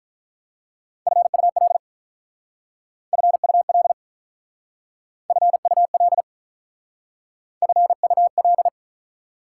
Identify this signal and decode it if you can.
Morse code, heard as a keyed tone, about 47 wpm, 690 Hz. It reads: FUL FUL FUL FUL